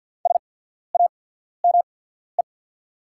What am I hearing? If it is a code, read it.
Morse code, heard as a keyed tone, about 48 wpm, spaced as for 14 wpm, 690 Hz. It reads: SAME
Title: 8x Character Spacing - Top 500 Words - 50wpm